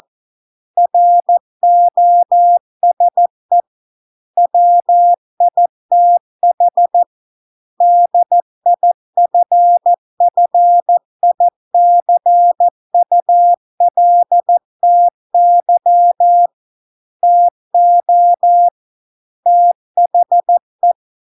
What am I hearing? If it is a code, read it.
Morse code, heard as a keyed tone, about 14 wpm, 700 Hz. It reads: ROSE WITH DIFFICULTY TO THE